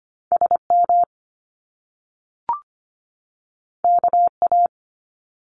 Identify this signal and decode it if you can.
Morse code, heard as a keyed tone, about 25 wpm, 700 Hz. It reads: SM KA
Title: Top Two Letter Combinations in Words 25wpm